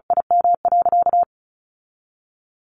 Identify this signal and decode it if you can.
Morse code, heard as a keyed tone, about 34 wpm, 700 Hz. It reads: IM.
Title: Sentences from Top 1100 Words 35wpm